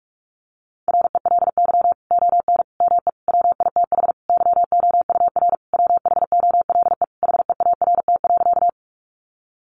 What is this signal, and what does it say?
Morse code, heard as a keyed tone, about 44 wpm, 700 Hz. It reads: RELY ON ME WITH YOUR WHOLE HEART.
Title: Sentences from Top 1200 Words 45wpm